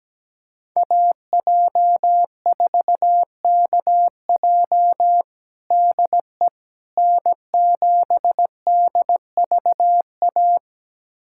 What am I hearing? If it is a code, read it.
Morse code, heard as a keyed tone, about 17 wpm, 700 Hz. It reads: AJ4KJ DE N7DVA
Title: USA Callsign DE Callsign - 5 Characters - 17wpm